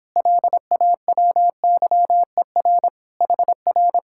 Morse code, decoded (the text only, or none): LAWYER HR